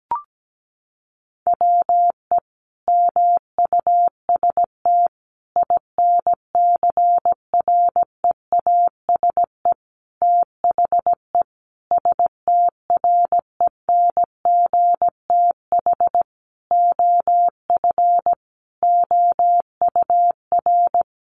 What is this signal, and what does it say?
Morse code, heard as a keyed tone, about 17 wpm, 700 Hz. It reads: WE MUST INCREASE THE STRENGTH OF OUR
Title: Sentences from Top 1300 Words 17wpm